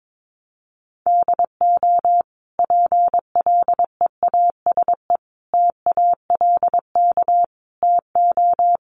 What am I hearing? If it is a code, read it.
Morse code, heard as a keyed tone, about 22 wpm, 700 Hz. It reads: DO PLEASE TALK TO
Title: Sentences from Top 1500 Words 22wpm